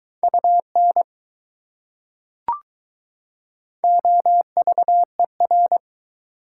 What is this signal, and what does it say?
Morse code, heard as a keyed tone, about 20 wpm, 700 Hz. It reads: UN OVER